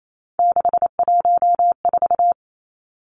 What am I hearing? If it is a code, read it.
Morse code, heard as a keyed tone, about 28 wpm, 700 Hz. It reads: 614
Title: Numbers - 3-Digit 28wpm